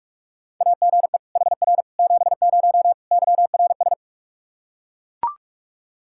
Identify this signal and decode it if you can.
Morse code, heard as a keyed tone, about 41 wpm, 700 Hz. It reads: AGE HR 70 YRS
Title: Practice QSO - Exchange 3 - More Info - 45wpm